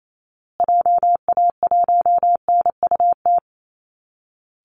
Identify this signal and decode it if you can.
Morse code, heard as a keyed tone, about 28 wpm, 700 Hz. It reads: JA1NUT
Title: Call Signs Repeated 2x - 28wpm